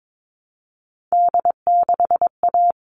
Morse code, heard as a keyed tone, about 22 wpm, 700 Hz. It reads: D6A
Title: All Cut Numbers - 3-Digit 22wpm